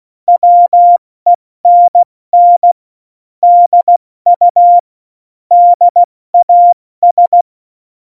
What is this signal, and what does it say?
Morse code, heard as a keyed tone, about 16 wpm, 700 Hz. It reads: WENN DU DAS